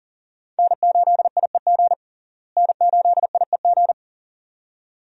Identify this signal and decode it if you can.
Morse code, heard as a keyed tone, about 40 wpm, 700 Hz. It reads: N8IEG N8IEG
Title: USA Call Signs - 5 Characters - Repeated 2x - 40wpm